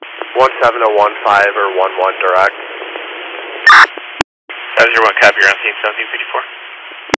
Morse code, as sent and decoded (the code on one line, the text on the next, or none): none